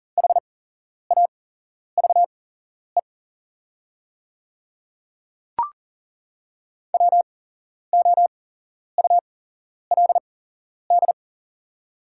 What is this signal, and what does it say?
Morse code, heard as a keyed tone, about 40 wpm, spaced as for 11 wpm, 700 Hz. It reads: HAVE WOULD
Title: 8x Character Spacing - Top 100 Words - 40wpm